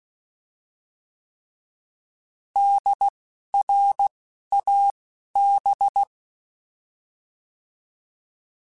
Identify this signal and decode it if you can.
Morse code, heard as a keyed tone, about 16 wpm, spaced as for 12 wpm, 800 Hz. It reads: DRAB
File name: Alphabetical / Alphabetical mp3s / Ds @10